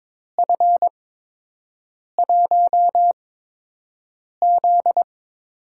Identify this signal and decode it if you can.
Morse code, heard as a keyed tone, about 22 wpm, spaced as for 6 wpm, 700 Hz. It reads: F1Z
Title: USA Call Signs - 4 Characters - 8x Character Spacing - 22wpm